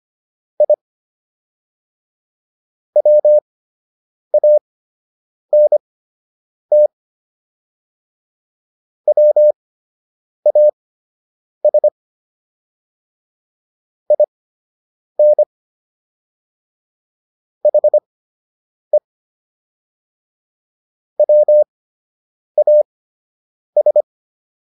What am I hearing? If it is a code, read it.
Morse code, heard as a keyed tone, about 25 wpm, spaced as for 8 wpm, 600 Hz. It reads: I WANT WAS IN HE WAS
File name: WD104 08